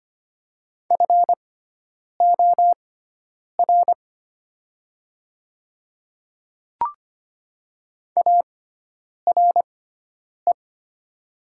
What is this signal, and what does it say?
Morse code, heard as a keyed tone, about 25 wpm, spaced as for 9 wpm, 700 Hz. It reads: FOR ARE
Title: 6x Character Spacing - Top 100 Words - 25wpm